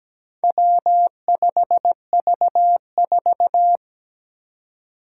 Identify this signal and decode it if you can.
Morse code, heard as a keyed tone, about 17 wpm, 700 Hz. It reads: W5V4
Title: Four Challenging Letter-Number Combinations 17wpm